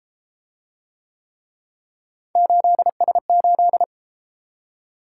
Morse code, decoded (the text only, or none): 8S8